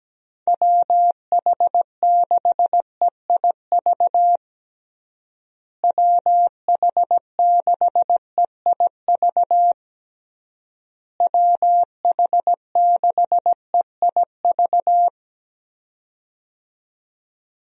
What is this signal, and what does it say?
Morse code, heard as a keyed tone, about 17 wpm, 700 Hz. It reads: WH6EIV WH6EIV WH6EIV
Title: USA Call Signs - 6 Characters - Repeated 3x - 17wpm